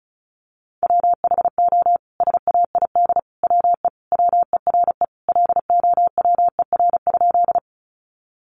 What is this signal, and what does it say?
Morse code, heard as a keyed tone, about 34 wpm, 700 Hz. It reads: WHO SAID WE WERE LOWER?
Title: Sentences from Top 700 Words 35wpm